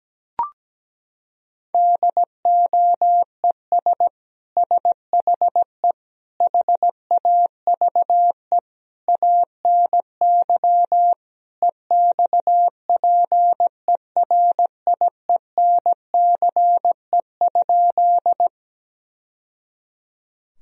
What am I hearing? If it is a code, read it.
Morse code, heard as a keyed tone, about 17 wpm, 700 Hz. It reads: DOES SHE HAVE ANY EXPERIENCE?
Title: Sentences from Top 1000 Words - Review Part 3 - 17wpm